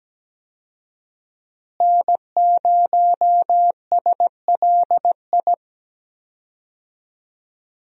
Morse code, heard as a keyed tone, about 17 wpm, 700 Hz. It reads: N0SLI
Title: USA Call Signs - 5 Characters - Repeated 2x - 17wpm